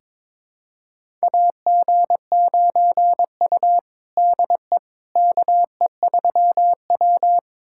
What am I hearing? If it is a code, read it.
Morse code, heard as a keyed tone, about 22 wpm, 700 Hz. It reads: AG9U DE KE3W
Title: USA Callsign DE Callsign - 4 Characters - 22wpm